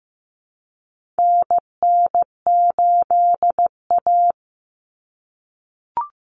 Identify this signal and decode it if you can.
Morse code, heard as a keyed tone, about 15 wpm, 700 Hz. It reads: NN8A
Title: Common Cut Numbers - 4-Digit 15wpm